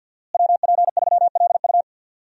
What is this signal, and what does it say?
Morse code, heard as a keyed tone, about 49 wpm, 700 Hz. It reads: WP3LU
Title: USA Call Signs - 5 Characters - Repeated 2x - 50wpm